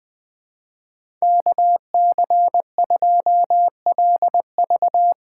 Morse code, heard as a keyed tone, about 20 wpm, 700 Hz. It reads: K C 2 L V
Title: USA Call Signs - 6x Character Spacing - 20wpm